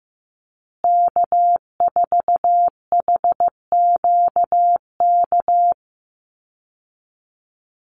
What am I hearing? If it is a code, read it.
Morse code, heard as a keyed tone, about 15 wpm, 700 Hz. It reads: K4HQK
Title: Call Signs Repeated 2x - 15wpm